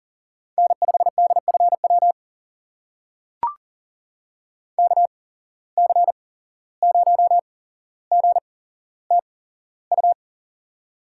Morse code, heard as a keyed tone, about 44 wpm, spaced as for 16 wpm, 690 Hz. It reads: N5DFW KC0GTU
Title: USA Call Signs - 8x Character Spacing - 40wpm